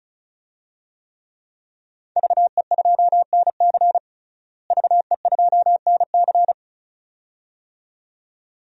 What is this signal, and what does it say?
Morse code, heard as a keyed tone, about 36 wpm, 700 Hz. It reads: VE2NC VE2NC
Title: DXCC-100 Call Signs - 5 Characters - Repeated 2x - 35wpm